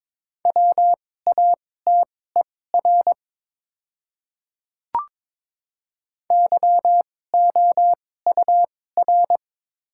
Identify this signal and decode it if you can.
Morse code, heard as a keyed tone, about 22 wpm, spaced as for 16 wpm, 700 Hz. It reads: WATER YOUR
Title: 2x Character Spacing - Top 300 Words - 22wpm